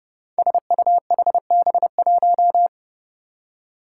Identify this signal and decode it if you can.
Morse code, heard as a keyed tone, about 28 wpm, 700 Hz. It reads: SUHB1
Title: Five Challenging Letter-Number Combinations 30wpm